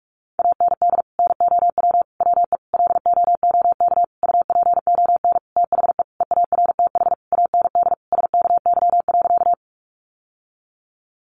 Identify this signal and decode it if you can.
Morse code, heard as a keyed tone, about 44 wpm, 700 Hz. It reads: AND NOW WE LOOK UPON THE EARTH AND SKY.
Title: Sentences from Top 1200 Words 45wpm